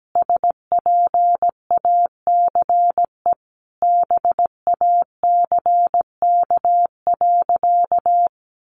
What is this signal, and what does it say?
Morse code, heard as a keyed tone, about 17 wpm, 700 Hz. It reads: SPACE BACK.
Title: Sentences from Top 1500 Words 17wpm